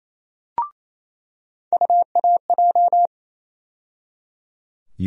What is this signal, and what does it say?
Morse code, heard as a keyed tone, about 28 wpm, 700 Hz. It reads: UAJ
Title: USA Call Sign Postfixes - 3 Letters 28wpm